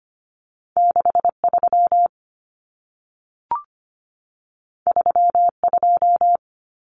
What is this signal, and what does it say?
Morse code, heard as a keyed tone, about 25 wpm, 700 Hz. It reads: 63 32
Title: Common Cut Numbers - 2-Digit 25wpm